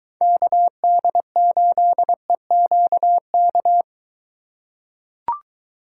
Morse code, heard as a keyed tone, about 21 wpm, 700 Hz. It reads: KD8EQK